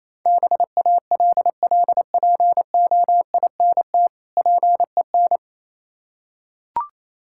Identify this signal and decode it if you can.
Morse code, heard as a keyed tone, about 27 wpm, 700 Hz. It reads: BALLPOINT PEN
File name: Easy Words - Speed-Racing with Hints - 22wpm